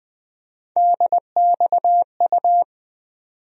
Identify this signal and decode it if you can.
Morse code, heard as a keyed tone, about 21 wpm, 700 Hz. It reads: DXU
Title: USA Call Sign Postfixes - 3 Letters 20wpm